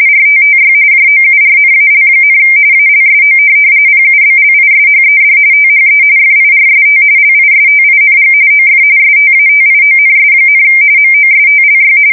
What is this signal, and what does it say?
RTTY, about 45.45 baud. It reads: RTTY JOURNAL, ALSO PUBLISHED THE FIRST LISTING OF STATIONS, MOSTLY
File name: fsk170 / teletype-RTTY